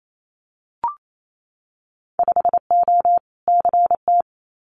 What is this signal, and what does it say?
Morse code, heard as a keyed tone, about 28 wpm, 700 Hz. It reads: 5O CT
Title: Winter Field Day Exchanges Encore 28wpm